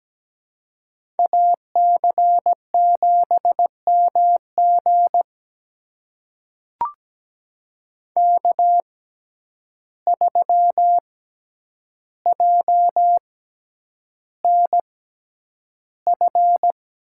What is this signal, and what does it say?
Morse code, heard as a keyed tone, about 17 wpm, spaced as for 9 wpm, 700 Hz. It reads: AC7MG K3JNF
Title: USA Call Signs - 5 Characters - 6x Character Spacing - 17wpm